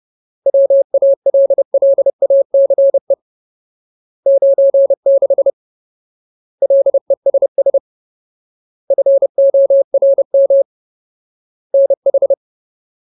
Morse code, heard as a keyed tone, about 30 wpm, 550 Hz. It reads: WALLACE 96 LESS FORM NH
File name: Ultimate Word List / 30WPM Ultimate